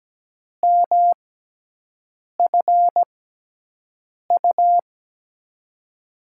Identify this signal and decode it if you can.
Morse code, heard as a keyed tone, about 17 wpm, spaced as for 6 wpm, 700 Hz. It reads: MFU
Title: USA Call Signs - 6x Character Spacing - 17wpm